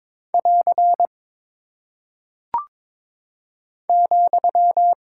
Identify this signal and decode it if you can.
Morse code, heard as a keyed tone, about 22 wpm, 700 Hz. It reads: <AR> ,
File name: practice / 22wpm / Single Punctuation-Prosign - 22wpm